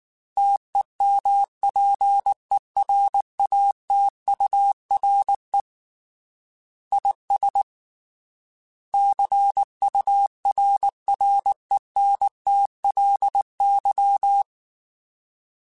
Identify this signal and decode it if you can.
Morse code, heard as a keyed tone, about 19 wpm, 800 Hz. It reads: TEMPERATURE IS CURRENTLY